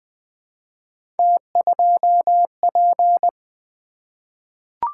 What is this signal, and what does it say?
Morse code, heard as a keyed tone, about 20 wpm, 700 Hz. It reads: T2P
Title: Three Letter-Number Combinations - 20wpm